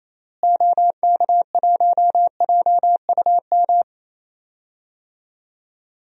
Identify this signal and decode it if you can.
Morse code, heard as a keyed tone, about 28 wpm, 700 Hz. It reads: OK1JUM
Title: DXCC-100 Call Signs - 6 Characters - Repeated 2x - 28wpm